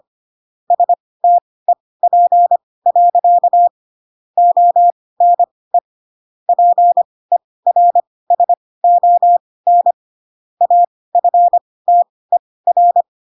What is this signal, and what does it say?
Morse code, heard as a keyed tone, about 25 wpm, spaced as for 18 wpm, 700 Hz. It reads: STEP. ONE PERSON AFTER
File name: Pinocchio ch3 12 25 ik1hge